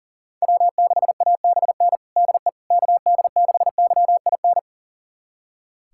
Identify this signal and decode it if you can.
Morse code, heard as a keyed tone, about 37 wpm, 700 Hz. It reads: W6ABN DE KD6YIN
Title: USA Callsign DE Callsign 40wpm